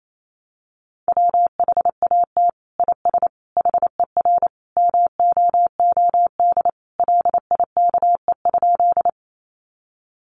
Morse code, heard as a keyed tone, about 28 wpm, 700 Hz. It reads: WHAT IS HER MOOD LIKE?
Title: Sentences from Top 1100 Words 28wpm